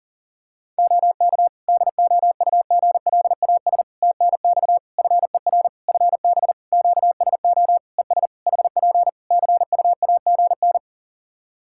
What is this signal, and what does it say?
Morse code, heard as a keyed tone, about 42 wpm, 700 Hz. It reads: OK DOUGLAS TNX FER FB QSO ES HP CUAGN
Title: Practice QSO - Exchange 4 - Ending - 40wpm